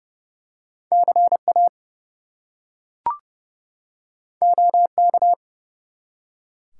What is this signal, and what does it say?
Morse code, heard as a keyed tone, about 30 wpm, 700 Hz. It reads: CA OK
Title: US States and Canadian Provinces 30wpm